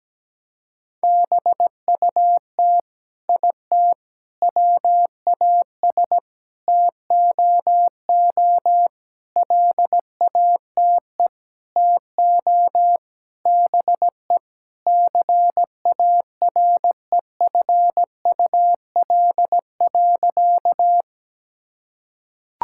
Morse code, heard as a keyed tone, about 17 wpm, 700 Hz. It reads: BUT IT WAS TOO LATE TO BE CAREFUL.
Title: Sentences from Top 1600 Words 17wpm